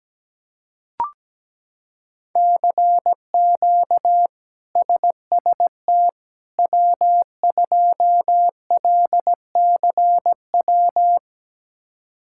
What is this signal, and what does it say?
Morse code, heard as a keyed tone, about 17 wpm, 700 Hz. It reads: CQ SST W2LCW
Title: K1USN SST QSO Practice - 17wpm